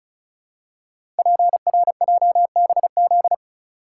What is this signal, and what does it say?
Morse code, heard as a keyed tone, about 35 wpm, 700 Hz. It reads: PRJBZ
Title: Code Groups - 5 Letters - 35wpm